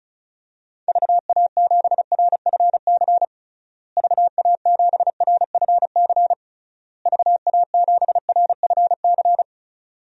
Morse code, heard as a keyed tone, about 36 wpm, 700 Hz. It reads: VA7RFC VA7RFC VA7RFC
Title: DXCC-100 Call Signs - 6 Characters - Repeated 3x - 35wpm